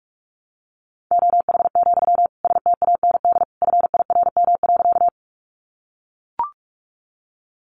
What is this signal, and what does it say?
Morse code, heard as a keyed tone, about 44 wpm, 700 Hz. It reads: OH, STAND FIRM.
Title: Sentences from Top 1400 Words 45wpm